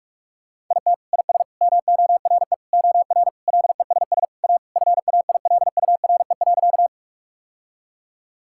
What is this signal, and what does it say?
Morse code, heard as a keyed tone, about 46 wpm, 700 Hz. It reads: IT IS MORE OR LESS A FAILURE.
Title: Sentences from Top 1000 Words - Review Part 6 - 45wpm